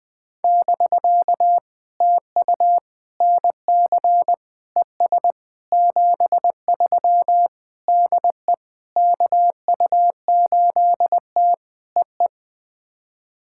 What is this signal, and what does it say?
Morse code, heard as a keyed tone, about 20 wpm, 700 Hz. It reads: <BK> TU NC ES 73 DE KU8T EE
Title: POTA QSO Practice - 20wpm